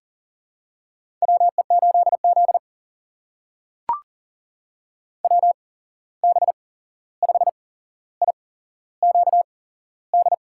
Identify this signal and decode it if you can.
Morse code, heard as a keyed tone, about 40 wpm, spaced as for 15 wpm, 700 Hz. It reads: WE8Z WB5IQD
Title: USA Call Signs - 8x Character Spacing - 40wpm